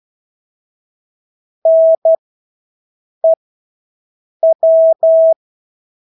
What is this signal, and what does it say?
Morse code, heard as a keyed tone, about 12 wpm, spaced as for 6 wpm, 650 Hz. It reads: NEW
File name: Engelske ord 12-6 wpm